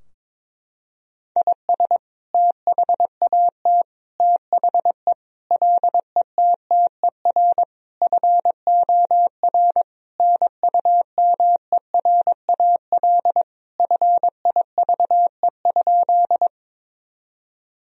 Morse code, heard as a keyed tone, about 22 wpm, 700 Hz. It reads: IS THAT THE LETTER FOR NUMERAL FIVE?
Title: Sentences from Top 1000 Words - Review Part 2 - 22wpm